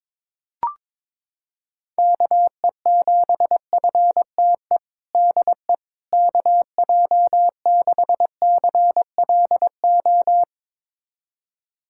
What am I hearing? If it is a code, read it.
Morse code, heard as a keyed tone, about 22 wpm, 700 Hz. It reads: KE7FTE DE KJ6CLO
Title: USA Callsign DE Callsign - 6 Characters - 22wpm